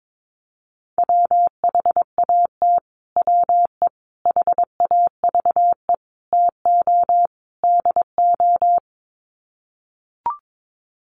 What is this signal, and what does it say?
Morse code, heard as a keyed tone, about 22 wpm, 700 Hz. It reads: WHAT WE HAVE TO DO
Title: Sets of 5 Words Encore 22wpm